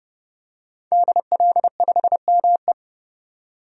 Morse code, heard as a keyed tone, about 31 wpm, 700 Hz. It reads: DL5ME
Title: DXCC-100 Call Signs - 5 Characters - 30wpm